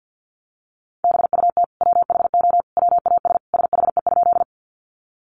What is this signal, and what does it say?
Morse code, heard as a keyed tone, about 50 wpm, 700 Hz. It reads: BUT WHO WAS SHE?
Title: Sentences from Top 100 Words 50wpm